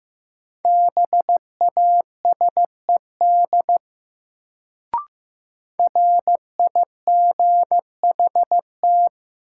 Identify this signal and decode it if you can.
Morse code, heard as a keyed tone, about 15 wpm, 700 Hz. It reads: BASED RIGHT
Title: Sets of 2 Words - New Word Drill - 15wpm